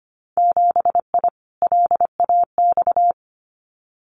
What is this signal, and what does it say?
Morse code, heard as a keyed tone, about 25 wpm, 700 Hz. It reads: 7I LAX
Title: Winter Field Day Exchanges Encore 25wpm